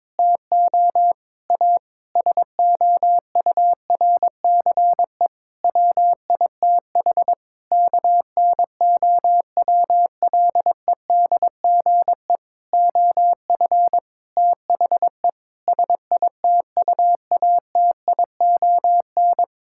Morse code, heard as a keyed tone, about 23 wpm, 700 Hz. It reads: TO A SOURCE WITH KNOWLEDGE OF THE SITUATION
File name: Sets of 9 Words - 22wpm